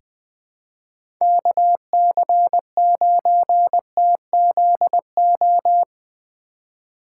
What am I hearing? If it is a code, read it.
Morse code, heard as a keyed tone, about 21 wpm, 700 Hz. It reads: KC9TZO
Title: K1USN SST QSO Practice - 20wpm